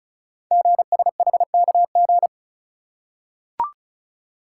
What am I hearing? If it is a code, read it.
Morse code, heard as a keyed tone, about 36 wpm, 700 Hz. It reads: GSHKG
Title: Code Groups - 5 Letters - 35wpm